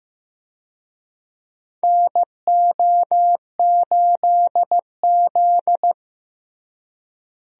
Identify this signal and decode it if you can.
Morse code, heard as a keyed tone, about 16 wpm, 700 Hz. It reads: NO8Z
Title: USA Call Signs - 4 Characters - Repeated 2x - 15wpm